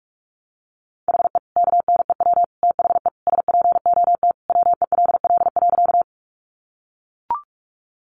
Morse code, heard as a keyed tone, about 44 wpm, 700 Hz. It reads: HE KNEW THE SPOT WELL.
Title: Sentences from Top 1100 Words 45wpm